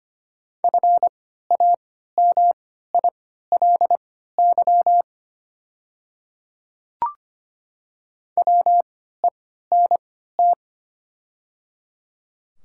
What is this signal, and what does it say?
Morse code, heard as a keyed tone, about 25 wpm, spaced as for 14 wpm, 700 Hz. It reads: FAMILY WENT
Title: 3x Character Spacing - Top 500 Words - 25wpm